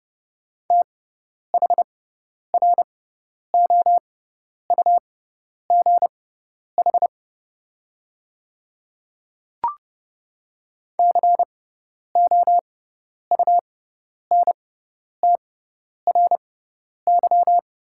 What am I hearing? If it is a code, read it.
Morse code, heard as a keyed tone, about 30 wpm, spaced as for 10 wpm, 690 Hz. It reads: THROUGH COUNTRY
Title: 6x Character Spacing - Top 500 Words - 30wpm